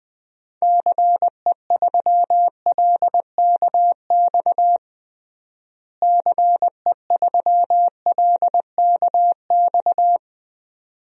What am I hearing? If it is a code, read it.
Morse code, heard as a keyed tone, about 20 wpm, 700 Hz. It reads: CE3LKX CE3LKX
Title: DXCC-100 Call Signs - 6 Characters - Repeated 2x - 20wpm